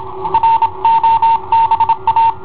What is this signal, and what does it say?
Morse code, heard as a keyed tone, about 27 wpm, 910 Hz. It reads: ROBA
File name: i7ohp / cassetta i7ohp B